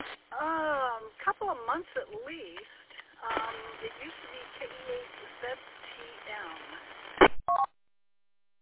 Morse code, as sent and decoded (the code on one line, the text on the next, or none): none